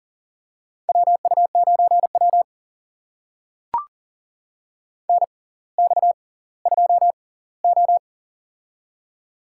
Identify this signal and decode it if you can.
Morse code, heard as a keyed tone, about 40 wpm, spaced as for 20 wpm, 700 Hz. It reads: WU9W NX2O
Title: USA Call Signs - 4 Characters - 6x Character Spacing - 40wpm